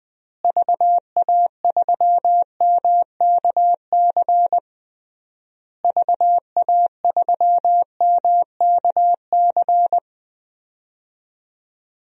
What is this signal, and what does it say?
Morse code, heard as a keyed tone, about 21 wpm, 700 Hz. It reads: VA3MKC VA3MKC
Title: DXCC-100 Call Signs - 6 Characters - Repeated 3x - 20wpm